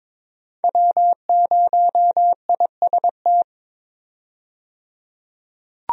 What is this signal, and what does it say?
Morse code, heard as a keyed tone, about 22 wpm, 700 Hz. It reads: W0IST
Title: USA Call Signs - 5 Characters - 22wpm